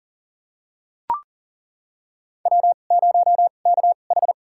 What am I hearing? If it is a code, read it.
Morse code, heard as a keyed tone, about 40 wpm, spaced as for 29 wpm, 700 Hz. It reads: W0KH
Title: USA Call Signs - 2x Character Spacing - 40wpm